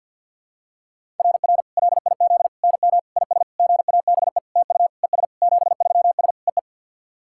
Morse code, heard as a keyed tone, about 50 wpm, 700 Hz. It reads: RR LIZ NM IS GABE TU ES 73S EE
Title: SOTA QSO Practice 50wpm